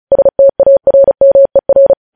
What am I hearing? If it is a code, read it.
Morse code, heard as a keyed tone, about 36 wpm, 550 Hz. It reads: STARMER
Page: News Headlines in Morse Code at 35 WPM